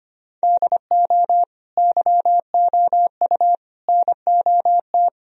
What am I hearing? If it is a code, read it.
Morse code, heard as a keyed tone, about 25 wpm, 700 Hz. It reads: DO YOU NOT
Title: Sentences from Top 1000 Words - Review Part 3 - 25wpm